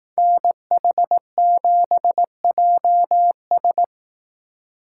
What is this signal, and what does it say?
Morse code, heard as a keyed tone, about 18 wpm, 700 Hz. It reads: NH7JS
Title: Speed Racing - USA 2x2 Call Signs 15wpm